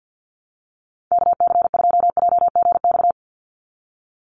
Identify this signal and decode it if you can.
Morse code, heard as a keyed tone, about 50 wpm, 700 Hz. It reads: KC2JGX
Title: USA Call Signs 50wpm